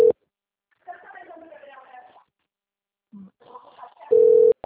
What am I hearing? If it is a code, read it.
Ringback tone (North America), 440 + 480 Hz.
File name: out- - - - - .47441